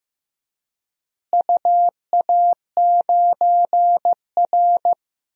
Morse code, heard as a keyed tone, about 15 wpm, 700 Hz. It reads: UA9R
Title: DXCC-100 Call Signs - 4 Characters - Repeated 3x - 15wpm